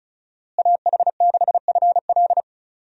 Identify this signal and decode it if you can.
Morse code, heard as a keyed tone, about 34 wpm, 700 Hz. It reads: AH6FL
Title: USA Call Signs - 5 Characters - Repeated 3x - 35wpm